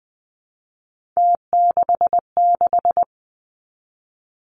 Morse code, heard as a keyed tone, about 20 wpm, 700 Hz. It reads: T66
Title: Three Challenging Letter-Number Combinations 20wpm